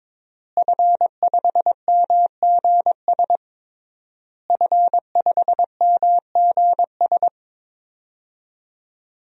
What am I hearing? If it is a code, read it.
Morse code, heard as a keyed tone, about 22 wpm, 700 Hz. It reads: F5MGS F5MGS
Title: DXCC-100 Call Signs - 5 Characters - Repeated 3x - 22wpm